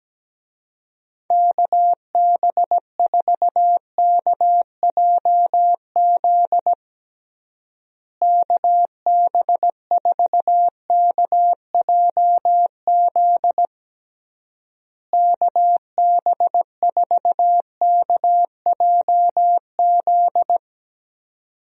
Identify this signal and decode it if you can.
Morse code, heard as a keyed tone, about 17 wpm, 700 Hz. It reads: KB4KJZ KB4KJZ KB4KJZ
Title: USA Call Signs - 6 Characters - Repeated 3x - 17wpm